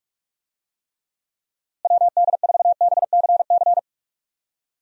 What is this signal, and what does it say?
Morse code, heard as a keyed tone, about 43 wpm, 700 Hz. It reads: WD4BCC
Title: USA Call Signs - 6 Characters - Repeated 3x - 45wpm